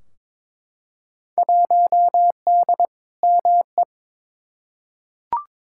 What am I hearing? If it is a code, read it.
Morse code, heard as a keyed tone, about 22 wpm, 700 Hz. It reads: 1D ME
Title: Field Day Exchanges Encore - 22wpm